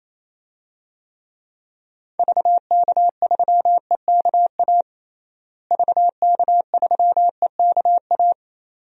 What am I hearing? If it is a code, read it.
Morse code, heard as a keyed tone, about 28 wpm, 700 Hz. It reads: VK3EKA VK3EKA
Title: DXCC-100 Call Signs - 6 Characters - Repeated 2x - 28wpm